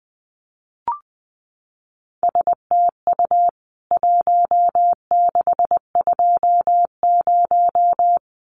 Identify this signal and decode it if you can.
Morse code, heard as a keyed tone, about 20 wpm, 700 Hz. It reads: STU 1620
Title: CWT Exchange 20wpm